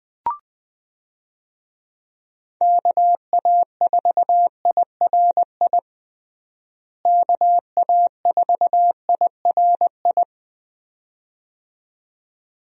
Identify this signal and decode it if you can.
Morse code, heard as a keyed tone, about 20 wpm, 700 Hz. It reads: KA4IRI KA4IRI
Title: USA Call Signs - 6 Characters - Repeated 2x - 20wpm